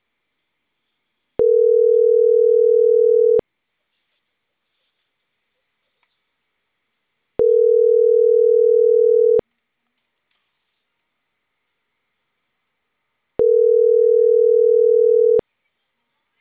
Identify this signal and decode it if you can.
Ringback tone (North America), 440 + 480 Hz.